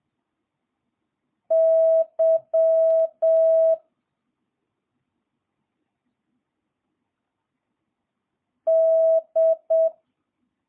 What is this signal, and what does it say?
Morse code, heard as a keyed tone, about 7 wpm, 650 Hz. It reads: Y D
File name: jh8cbh / 2022cw